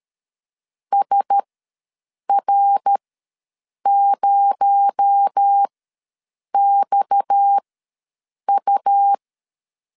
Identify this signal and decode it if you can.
Morse code, heard as a keyed tone, about 13 wpm, spaced as for 7 wpm, 800 Hz. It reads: SR0XU
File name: LEZIONI / 9 2 ws 12wpm 3406